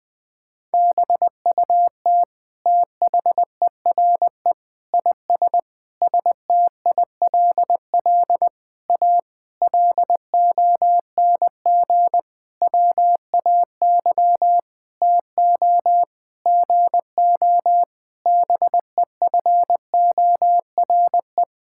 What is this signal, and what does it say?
Morse code, heard as a keyed tone, about 20 wpm, 700 Hz. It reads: BUT THERE IS STILL A LONG WAY TO GO BEFORE
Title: Sentences with 9-Word Phrases - 20wpm